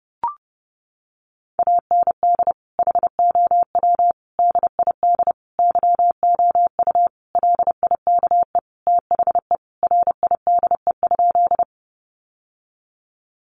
Morse code, heard as a keyed tone, about 30 wpm, 700 Hz. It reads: AND HOW DID YOU LIKE THE RIDE?
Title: Sentences from Top 1200 Words 30wpm